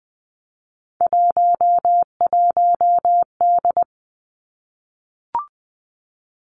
Morse code, heard as a keyed tone, about 20 wpm, 700 Hz. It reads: 11D
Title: Three Challenging Letter-Number Combinations 20wpm